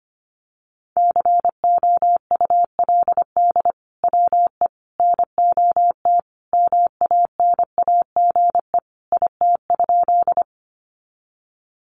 Morse code, heard as a keyed tone, about 25 wpm, 700 Hz. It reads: COULD WE NOT MANAGE IT?
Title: Sentences from Top 1400 Words 25wpm